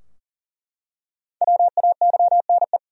Morse code, heard as a keyed tone, about 39 wpm, 700 Hz. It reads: WAYNE
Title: USA Common Names - 40wpm